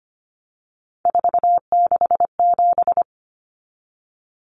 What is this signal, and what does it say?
Morse code, heard as a keyed tone, about 25 wpm, 700 Hz. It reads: 467
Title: Common Cut Numbers - 3-Digit 25wpm